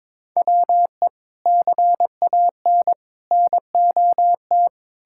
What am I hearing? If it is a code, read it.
Morse code, heard as a keyed tone, about 22 wpm, 700 Hz. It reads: WE CAN NOT
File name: practice / 22wpm / Sets of 8 Words - 22wpm